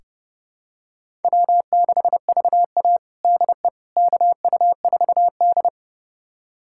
Morse code, heard as a keyed tone, about 30 wpm, 700 Hz. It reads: W6VA DE KU4D
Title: USA Callsign DE Callsign - 4 Characters - 30wpm